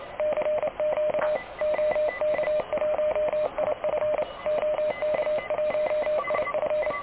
Morse code, heard as a keyed tone, about 30 wpm, 610 Hz, with radio noise. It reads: CQ OK1IF OK1IF
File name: truSDX / KVPA ruSDXC